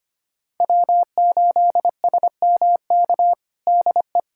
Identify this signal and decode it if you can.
Morse code, heard as a keyed tone, about 25 wpm, 700 Hz. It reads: W8SMK DE
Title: USA Callsign DE Callsign - 5 Characters - 25wpm